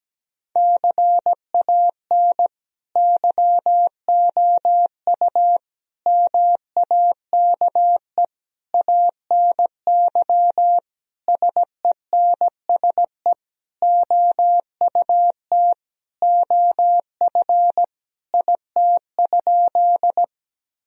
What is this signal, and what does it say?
Morse code, heard as a keyed tone, about 17 wpm, 700 Hz. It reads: CAN YOU MAKE ANY SENSE OUT OF IT?
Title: Sentences from Top 1000 Words - Review Part 4 - 17wpm